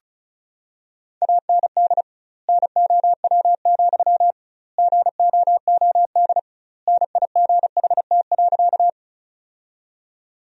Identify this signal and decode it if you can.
Morse code, heard as a keyed tone, about 34 wpm, 700 Hz. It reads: AND NOW, GOOD NIGHT.
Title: 2x Word-Spacing - Sentences from Top 300 Words - 35wpm